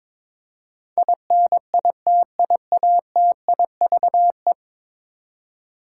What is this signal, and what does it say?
Morse code, heard as a keyed tone, about 22 wpm, 700 Hz. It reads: INITIATIVE
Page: morse-code-ninja-practice-22wpm